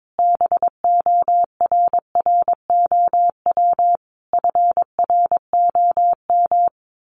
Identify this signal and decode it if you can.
Morse code, heard as a keyed tone, about 22 wpm, 700 Hz. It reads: BORROW FROM
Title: Sentences from Top 1500 Words 22wpm